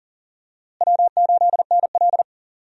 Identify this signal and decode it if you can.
Morse code, heard as a keyed tone, about 40 wpm, 700 Hz. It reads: W8NL
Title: USA Call Signs - 4 Characters - Repeated 3x - 40wpm